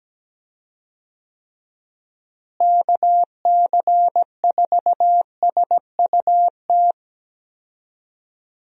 Morse code, heard as a keyed tone, about 17 wpm, 700 Hz. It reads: KC4SUT
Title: USA Call Signs - 6 Characters - Repeated 2x - 17wpm